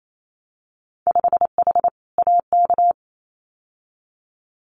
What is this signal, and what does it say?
Morse code, heard as a keyed tone, about 28 wpm, 700 Hz. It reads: 5H AK
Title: Winter Field Day Exchanges Encore 28wpm